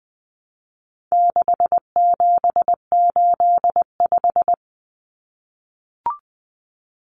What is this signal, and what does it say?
Morse code, heard as a keyed tone, about 20 wpm, 700 Hz. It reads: 6785
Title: Common Cut Numbers - 4-Digit 20wpm